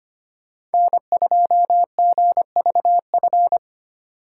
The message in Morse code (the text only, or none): N2GVF